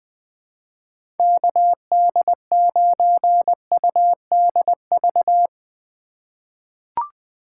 Morse code, heard as a keyed tone, about 20 wpm, 700 Hz. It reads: KD9UDV
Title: 2023 POTA Activator Call Signs - 20wpm